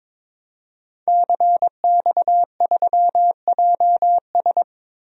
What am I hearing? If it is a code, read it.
Morse code, heard as a keyed tone, about 22 wpm, 700 Hz. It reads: CX3JS
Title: DXCC-100 Call Signs - 5 Characters - Repeated 2x - 22wpm